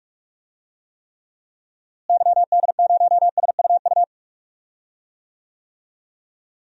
Morse code, heard as a keyed tone, about 45 wpm, 700 Hz. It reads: YD0SUU
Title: DXCC-100 Call Signs - 6 Characters - Repeated 2x - 45wpm